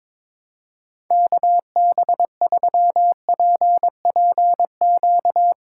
Morse code, heard as a keyed tone, about 22 wpm, 700 Hz. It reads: KB3PPQ
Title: USA Call Signs - 6 Characters - 3x Character Spacing - 22wpm